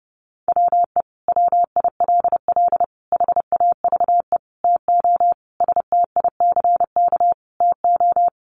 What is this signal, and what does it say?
Morse code, heard as a keyed tone, about 30 wpm, 700 Hz. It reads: WE WILL HAVE TO STICK TO
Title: Sentences from Top 1200 Words 30wpm